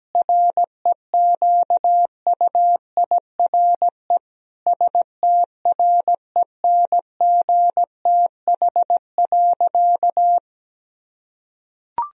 Morse code, heard as a keyed tone, about 17 wpm, 700 Hz. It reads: REQUIRE STRENGTH.
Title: Sentences from Top 1000 Words - Review Part 4 - 17wpm